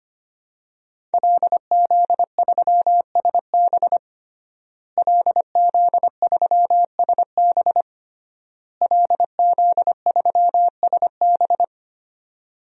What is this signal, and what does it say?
Morse code, heard as a keyed tone, about 25 wpm, 700 Hz. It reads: LZ3SB LZ3SB LZ3SB
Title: DXCC-100 Call Signs - 5 Characters - Repeated 3x - 25wpm